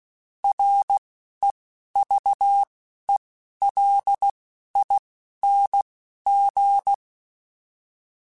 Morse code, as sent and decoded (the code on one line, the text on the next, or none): .-. / . / ...- / . / .-.. / .. / -. / --.
R E V E L I N G